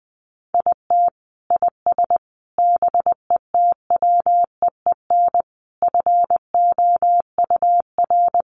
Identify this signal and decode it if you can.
Morse code, heard as a keyed tone, about 20 wpm, 700 Hz. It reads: IT IS BETWEEN FOUR
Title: Sentences from Top 1500 Words 20wpm